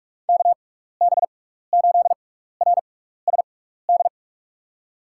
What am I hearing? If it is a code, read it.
Morse code, heard as a keyed tone, about 46 wpm, spaced as for 15 wpm, 700 Hz. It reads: KB8RSD
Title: USA Call Signs - 6x Character Spacing - 45wpm